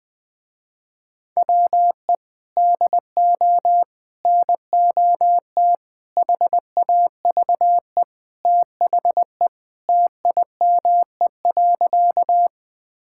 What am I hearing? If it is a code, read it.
Morse code, heard as a keyed tone, about 20 wpm, 700 Hz. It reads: WE DO NOT HAVE THE TIME.
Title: Sentences with 5-Word Phrases - Spoken Beginning - 20wpm